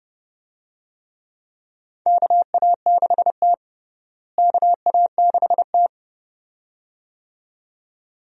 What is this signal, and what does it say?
Morse code, heard as a keyed tone, about 30 wpm, 700 Hz. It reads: KA6T KA6T
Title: USA Call Signs - 4 Characters - Repeated 2x - 30wpm